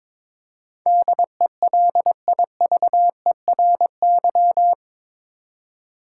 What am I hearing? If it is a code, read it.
Morse code, heard as a keyed tone, about 22 wpm, 700 Hz. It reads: DELIVERY
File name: practice / 22wpm / Eight Letter Words - 22wpm